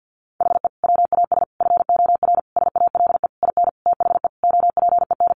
Morse code, heard as a keyed tone, about 50 wpm, 700 Hz. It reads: HE WAS FOR SALE IN THE OPEN
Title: Sentences from Top 1400 Words 50wpm